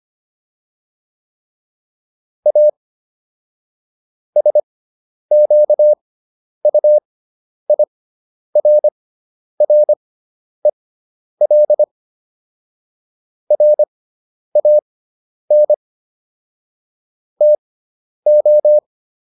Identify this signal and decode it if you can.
Morse code, heard as a keyed tone, about 25 wpm, spaced as for 10 wpm, 600 Hz. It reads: A SQUIRREL RAN TO